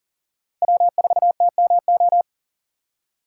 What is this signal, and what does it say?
Morse code, heard as a keyed tone, about 40 wpm, 700 Hz. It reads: W4TMO
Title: K1USN SST QSO Practice - 40wpm